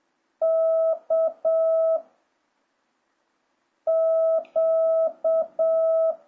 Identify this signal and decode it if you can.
Morse code, heard as a keyed tone, about 7 wpm, 660 Hz. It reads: K Q